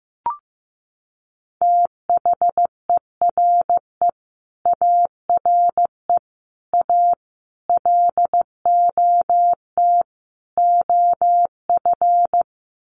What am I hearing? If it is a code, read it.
Morse code, heard as a keyed tone, about 15 wpm, 700 Hz. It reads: THERE ARE A LOT OF
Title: Sets of 5 Words 15wpm